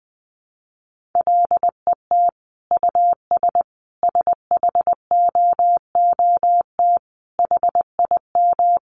Morse code, heard as a keyed tone, about 20 wpm, 700 Hz. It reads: LET US SHOOT HIM
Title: Sentences from Top 1200 Words 20wpm